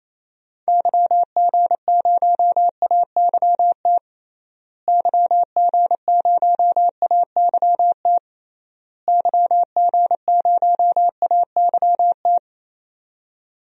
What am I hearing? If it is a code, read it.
Morse code, heard as a keyed tone, about 28 wpm, 700 Hz. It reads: YG0AYT YG0AYT YG0AYT
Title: DXCC-100 Call Signs - 6 Characters - Repeated 3x - 28wpm